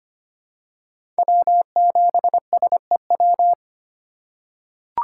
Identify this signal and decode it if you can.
Morse code, heard as a keyed tone, about 25 wpm, 700 Hz. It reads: W7SEW
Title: USA Call Signs - 6x Character Spacing - 25wpm